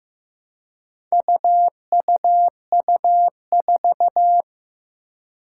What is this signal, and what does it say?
Morse code, heard as a keyed tone, about 15 wpm, 700 Hz. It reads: UUU4
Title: Four Challenging Letter-Number Combinations 15wpm